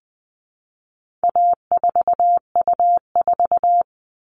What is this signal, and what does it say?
Morse code, heard as a keyed tone, about 20 wpm, 700 Hz. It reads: A4U4
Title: All Cut Numbers - 4-Digit 20wpm